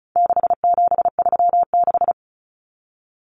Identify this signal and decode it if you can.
Morse code, heard as a keyed tone, about 34 wpm, 700 Hz. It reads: 6736
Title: Common Cut Numbers - 4-Digit 35wpm